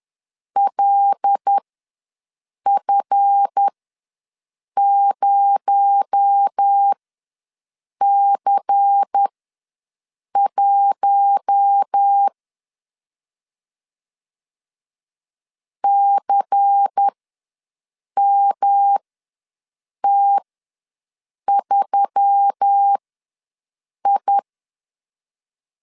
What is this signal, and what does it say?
Morse code, heard as a keyed tone, about 10 wpm, spaced as for 6 wpm, 800 Hz. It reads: LF0C1 CMT3I
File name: LEZIONI / 7 GF9Z / 7 GF ws 10wpm 3918